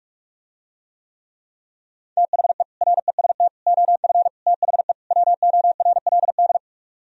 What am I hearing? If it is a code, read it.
Morse code, heard as a keyed tone, about 50 wpm, 700 Hz. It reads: THE REST OF THE WORLD